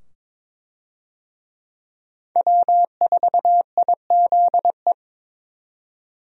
Morse code, heard as a keyed tone, about 21 wpm, 700 Hz. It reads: W4IZE
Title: USA Call Signs - 5 Characters - 22wpm